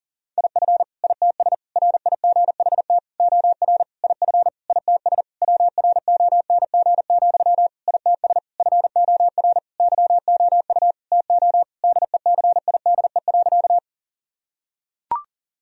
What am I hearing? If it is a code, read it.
Morse code, heard as a keyed tone, about 40 wpm, 700 Hz. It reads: IF ITS RIGHT OR IF ITS WRONG, ITS FOR YOU TO DECIDE.
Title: Sentences from Top 1000 Words - Review Part 3 - 40wpm